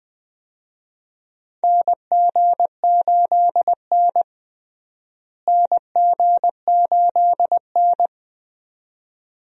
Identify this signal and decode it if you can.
Morse code, heard as a keyed tone, about 20 wpm, 700 Hz. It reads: NG8N NG8N
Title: USA Call Signs - 4 Characters - Repeated 2x - 20wpm